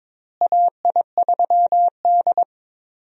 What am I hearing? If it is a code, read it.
Morse code, heard as a keyed tone, about 22 wpm, 700 Hz. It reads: AI3D
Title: USA Call Signs - 4 Characters - Repeated 3x - 22wpm